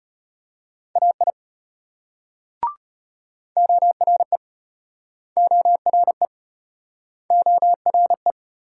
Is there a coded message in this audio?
Yes: Morse code from 1.0 to 8.3 s, heard as a keyed tone, about 35 wpm, 700 Hz. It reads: AI ORE ORE ORE